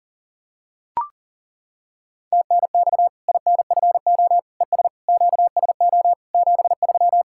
Morse code, heard as a keyed tone, about 40 wpm, 700 Hz. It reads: TNX INFO ES QSO 73
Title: Practice QSO - Exchange 4 - Ending - 40wpm